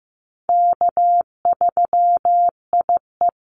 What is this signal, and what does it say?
Morse code, heard as a keyed tone, about 15 wpm, 700 Hz. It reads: K3IE
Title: Call Signs Repeated 3x - 15wpm